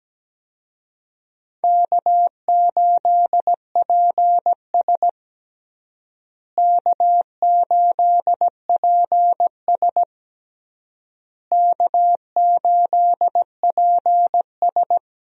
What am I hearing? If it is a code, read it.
Morse code, heard as a keyed tone, about 17 wpm, 700 Hz. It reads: K8PS K8PS K8PS
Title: USA Call Signs - 4 Characters - Repeated 3x - 17wpm